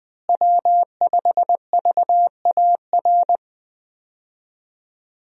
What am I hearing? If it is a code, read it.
Morse code, heard as a keyed tone, about 21 wpm, 700 Hz. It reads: W5VAR
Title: USA Call Signs - 5 Characters - Repeated 3x - 20wpm